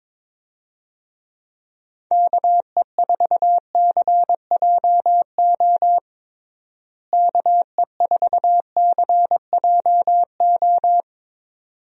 Morse code, heard as a keyed tone, about 22 wpm, 700 Hz. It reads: KE4CJO KE4CJO
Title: USA Call Signs - 6 Characters - Repeated 2x - 22wpm